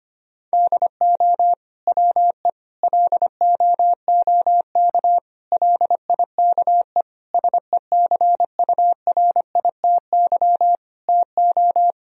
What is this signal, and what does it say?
Morse code, heard as a keyed tone, about 25 wpm, 700 Hz. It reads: DO WE LOOK LIKE SECURITY TO
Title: Sentences from Top 1000 Words - Review Part 4 - 25wpm